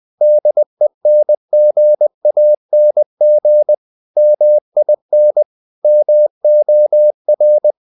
Morse code, heard as a keyed tone, about 20 wpm, 600 Hz. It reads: DENGANG MIN MOR
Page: Ferie 20 wpm